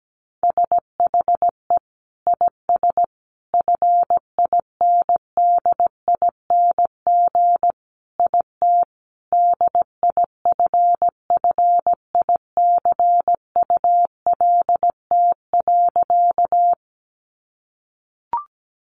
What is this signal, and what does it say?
Morse code, heard as a keyed tone, about 17 wpm, 700 Hz. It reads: SHE IS FINDING IT DIFFICULT.
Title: Sentences from Top 1100 Words 17wpm